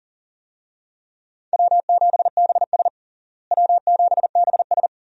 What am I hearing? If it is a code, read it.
Morse code, heard as a keyed tone, about 40 wpm, 700 Hz. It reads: W7BS W7BS
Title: USA Call Signs - 4 Characters - Repeated 2x - 40wpm